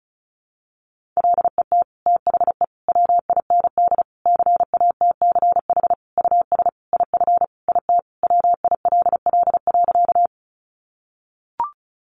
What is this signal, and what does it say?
Morse code, heard as a keyed tone, about 34 wpm, 700 Hz. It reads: LET THE WIND CATCH US IF IT WILL.
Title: Sentences from Top 1500 Words 35wpm